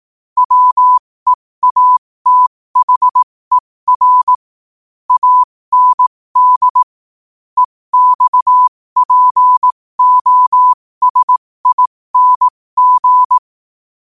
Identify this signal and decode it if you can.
Morse code, heard as a keyed tone, about 18 wpm, spaced as for 15 wpm, 1000 Hz. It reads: WEATHER AND EXPOSING
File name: qotdfiles / 15wpm